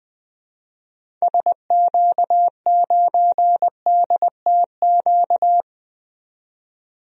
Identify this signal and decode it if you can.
Morse code, heard as a keyed tone, about 20 wpm, 700 Hz. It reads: SQ9DTQ
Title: DXCC-100 Call Signs - 6 Characters - 20wpm